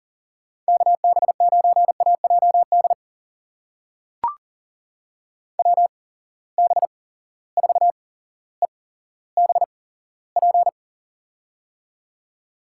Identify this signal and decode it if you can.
Morse code, heard as a keyed tone, about 40 wpm, spaced as for 17 wpm, 700 Hz. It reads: KB9AJD WB4EBP
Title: USA Call Signs - 8x Character Spacing - 40wpm